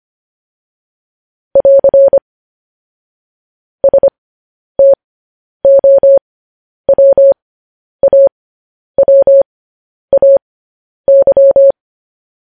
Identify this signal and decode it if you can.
Morse code, heard as a keyed tone, about 25 wpm, spaced as for 10 wpm, 550 Hz. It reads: <AR> STOWAWAY